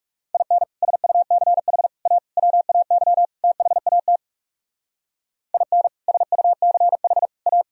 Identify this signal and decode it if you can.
Morse code, heard as a keyed tone, about 36 wpm, 700 Hz. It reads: IN SUCH A WAY THAT IN SUCH A
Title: Speed-Racing - Sets of 5 Words 30wpm